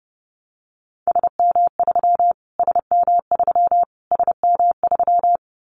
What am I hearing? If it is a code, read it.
Morse code, heard as a keyed tone, about 30 wpm, 700 Hz. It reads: SM3 SM3 SM3
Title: Callsign Prefixes for Top 25 DXCC - Repeated 3x - 30wpm